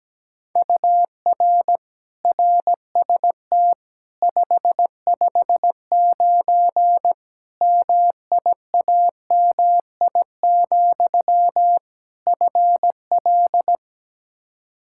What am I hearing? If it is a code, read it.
Morse code, heard as a keyed tone, about 17 wpm, 700 Hz. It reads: UR RST 559 MIAMI, FL
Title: K1USN SST QSO Practice 17wpm